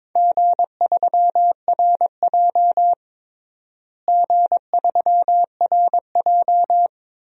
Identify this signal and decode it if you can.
Morse code, heard as a keyed tone, about 22 wpm, 700 Hz. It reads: G3RJ G3RJ
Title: DXCC-100 Call Signs - 4 Characters - Repeated 2x - 22wpm